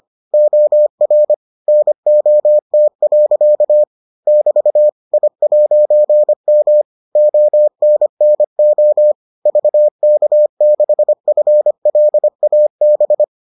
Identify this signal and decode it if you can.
Morse code, heard as a keyed tone, about 25 wpm, 600 Hz. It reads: OR NOT. <BT> I'M ONNO VK6FLAB